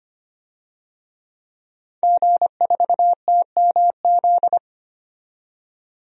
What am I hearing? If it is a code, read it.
Morse code, heard as a keyed tone, about 25 wpm, 700 Hz. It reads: G4TMZ
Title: DXCC-100 Call Signs - 5 Characters - 25wpm